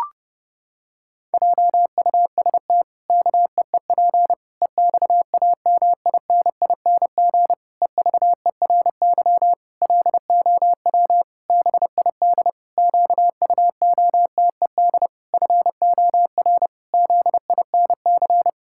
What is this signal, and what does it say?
Morse code, heard as a keyed tone, about 30 wpm, 700 Hz. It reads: JUST KEEP EXAMINING EVERY LOW BID QUOTED FOR ZINC
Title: Pangram Challenge v2 - 30wpm